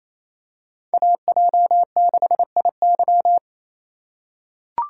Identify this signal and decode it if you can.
Morse code, heard as a keyed tone, about 28 wpm, 700 Hz. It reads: AJ6IY
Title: USA Call Signs - 5 Characters - 2x Character Spacing - 28wpm